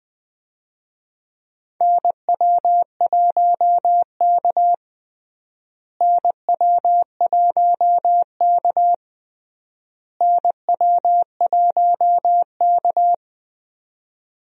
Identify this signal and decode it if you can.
Morse code, heard as a keyed tone, about 20 wpm, 700 Hz. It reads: NW1K NW1K NW1K
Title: USA Call Signs - 4 Characters - Repeated 3x - 20wpm